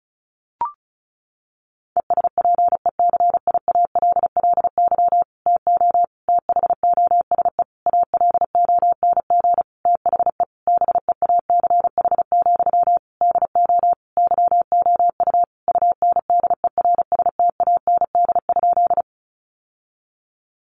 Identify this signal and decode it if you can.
Morse code, heard as a keyed tone, about 34 wpm, 700 Hz. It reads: ESPECIALLY TO THOSE ALONG THE BEACH, DO YOU UNDERSTAND?
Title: Sentences from Top 1500 Words 35wpm